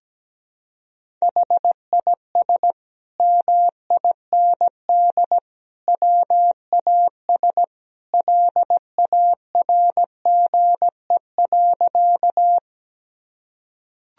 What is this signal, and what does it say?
Morse code, heard as a keyed tone, about 17 wpm, 700 Hz. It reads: HIS MIND WAS LARGE.
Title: Sentences from Top 1000 Words - Review Part 1 - 17wpm